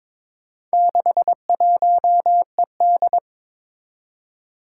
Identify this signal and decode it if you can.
Morse code, heard as a keyed tone, about 22 wpm, 700 Hz. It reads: 61ED
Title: Four Challenging Letter-Number Combinations 22wpm